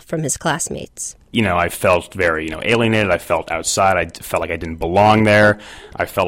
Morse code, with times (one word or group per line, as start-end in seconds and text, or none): none